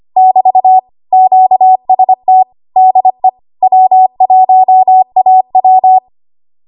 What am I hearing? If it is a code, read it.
Morse code, heard as a keyed tone, about 25 wpm, 750 Hz. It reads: <BT> QST DE W1AW
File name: Morse / 25WPM